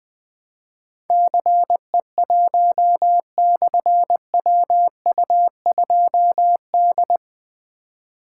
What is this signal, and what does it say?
Morse code, heard as a keyed tone, about 20 wpm, 700 Hz. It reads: CE1/WU2D
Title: Difficult Call Signs 20wpm